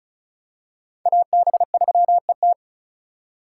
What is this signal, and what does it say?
Morse code, heard as a keyed tone, about 36 wpm, 700 Hz. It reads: AB3ET
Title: USA Call Signs - 4x Character Spacing - 35wpm